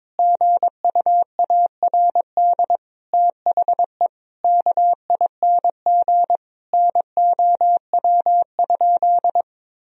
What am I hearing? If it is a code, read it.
Morse code, heard as a keyed tone, about 22 wpm, 700 Hz. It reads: GUARD THE KING NOW?
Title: Sentences from Top 1000 Words - Review Part 4 - 22wpm